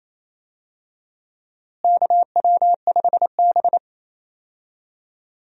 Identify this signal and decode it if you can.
Morse code, heard as a keyed tone, about 28 wpm, 700 Hz. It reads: KW5B
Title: USA Call Signs - 4 Characters - Repeated 3x - 28wpm